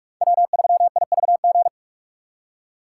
Morse code, heard as a keyed tone, about 46 wpm, 700 Hz. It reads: W3IVG
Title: USA Call Signs - 5 Characters - 2x Character Spacing - 45wpm